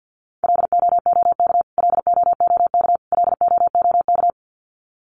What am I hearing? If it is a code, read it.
Morse code, heard as a keyed tone, about 50 wpm, 700 Hz. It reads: LOOK LOOK LOOK
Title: Top 300 Words Repeated 50wpm